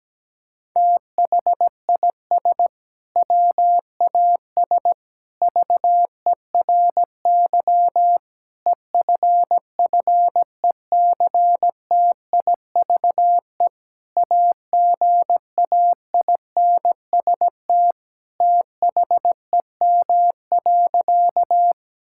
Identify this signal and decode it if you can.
Morse code, heard as a keyed tone, about 17 wpm, 700 Hz. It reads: THIS WAS VERY EFFECTIVE AGAINST THEM.
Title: Sentences from Top 1000 Words - Review Part 6 - 17wpm